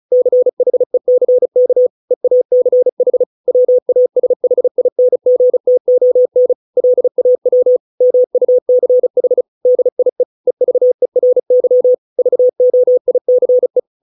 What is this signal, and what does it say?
Morse code, heard as a keyed tone, about 34 wpm, 500 Hz. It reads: CHECK EACH WASHINGTON LAW MUCH DIE EVERY VOICE